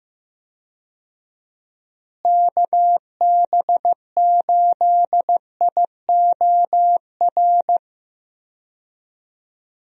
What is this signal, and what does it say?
Morse code, heard as a keyed tone, about 15 wpm, 700 Hz. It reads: KB8IOR
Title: USA Call Signs - 6 Characters - Repeated 3x - 15wpm